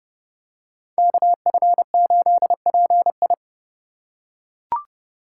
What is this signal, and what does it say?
Morse code, heard as a keyed tone, about 30 wpm, 700 Hz. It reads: KF8PI
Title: USA Call Signs - 5 Characters - 3x Character Spacing - 30wpm